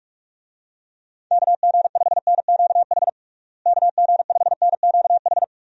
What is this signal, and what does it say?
Morse code, heard as a keyed tone, about 46 wpm, 700 Hz. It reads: KG5NQH KG5NQH
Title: USA Call Signs - 6 Characters - Repeated 2x - 45wpm